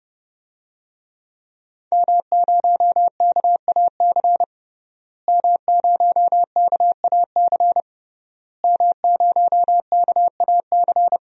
Morse code, heard as a keyed tone, about 30 wpm, 700 Hz. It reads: M0KAC M0KAC M0KAC
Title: DXCC-100 Call Signs - 5 Characters - Repeated 3x - 30wpm